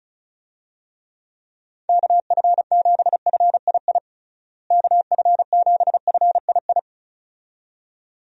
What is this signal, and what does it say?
Morse code, heard as a keyed tone, about 34 wpm, 700 Hz. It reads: KF7FII KF7FII
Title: USA Call Signs - 6 Characters - Repeated 2x - 35wpm